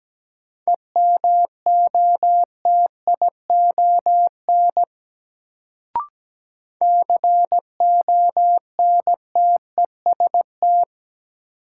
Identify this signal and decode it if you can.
Morse code, heard as a keyed tone, about 17 wpm, 700 Hz. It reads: EMOTION CONTEST
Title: Sentences from Top 1600 Words - New Word Drill Alt - 17wpm